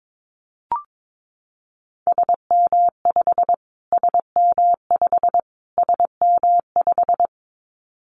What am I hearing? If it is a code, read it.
Morse code, heard as a keyed tone, about 22 wpm, 700 Hz. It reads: SM5 SM5 SM5
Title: Callsign Prefixes for Top 25 DXCC - Repeated 3x - 22wpm